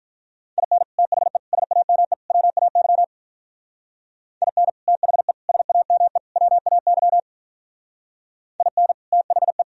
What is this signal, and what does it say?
Morse code, heard as a keyed tone, about 43 wpm, 700 Hz. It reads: IN THE SAME WAY IN THE SAME WAY IN THE
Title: Speed-Racing - Sets of 4 Words 35wpm